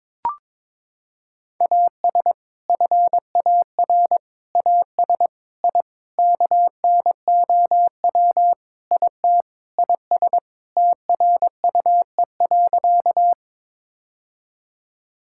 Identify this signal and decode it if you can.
Morse code, heard as a keyed tone, about 21 wpm, 700 Hz. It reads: AS FAR AS I KNOW IT IS TRUE.
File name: practice / 22wpm / Sentences with 4-Word Phrases - 22wpm